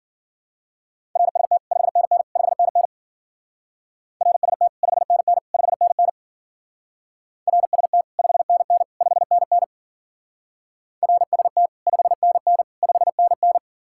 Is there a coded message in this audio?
Yes: Morse code from 1.2 to 13.6 s, heard as a keyed tone, about 51 wpm, 700 Hz. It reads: RST 5NN 5NN RST 5NN 5NN RST 5NN 5NN RST 5NN 5NN